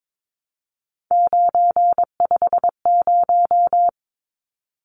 Morse code, heard as a keyed tone, about 22 wpm, 700 Hz. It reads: 950
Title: Numbers - 3-Digit 22wpm